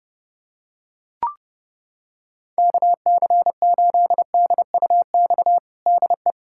Morse code, heard as a keyed tone, about 32 wpm, 700 Hz. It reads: KC8DUX DE
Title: USA Callsign DE Callsign 30wpm